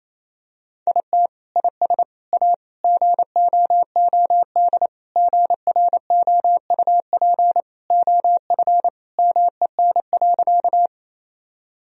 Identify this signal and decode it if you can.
Morse code, heard as a keyed tone, about 28 wpm, 700 Hz. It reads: IT IS A GOOD GROUP OF MEN.
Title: Sentences from Top 1000 Words - Review Part 1 - 28wpm